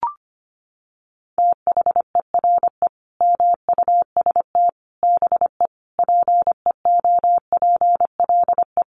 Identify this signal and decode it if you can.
Morse code, heard as a keyed tone, about 25 wpm, 700 Hz. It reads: THERE MUST BE PEOPLE
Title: Sentences from Top 1400 Words 25wpm